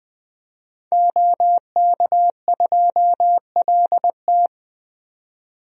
Morse code, heard as a keyed tone, about 20 wpm, 700 Hz. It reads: OK2LT
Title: DXCC-100 Call Signs - 5 Characters - 20wpm